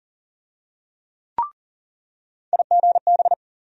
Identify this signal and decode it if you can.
Morse code, heard as a keyed tone, about 40 wpm, 700 Hz. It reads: IGB
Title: USA Call Sign Postfixes - 3 Letters 40wpm